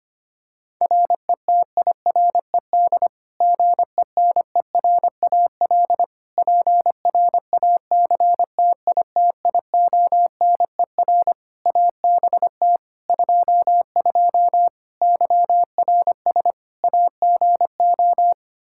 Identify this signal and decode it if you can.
Morse code, heard as a keyed tone, about 25 wpm, 700 Hz. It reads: RETIRED GENERAL PRACTITIONER ABT 22 YRS AGO
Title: Practice QSO - Exchange 3 - More Info - 25wpm